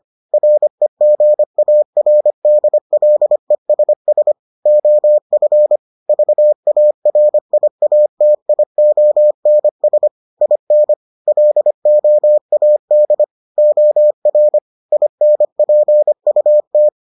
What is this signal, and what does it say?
Morse code, heard as a keyed tone, about 25 wpm, 600 Hz. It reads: REGARDLESS OF VARIATIONS IN LOAD OR INPUT